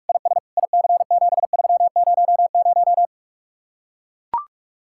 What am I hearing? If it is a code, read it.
Morse code, heard as a keyed tone, about 45 wpm, 700 Hz. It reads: IS IC7300